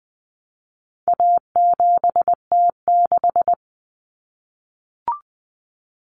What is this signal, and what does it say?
Morse code, heard as a keyed tone, about 20 wpm, 700 Hz. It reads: A7T6
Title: Common Cut Numbers - 4-Digit 20wpm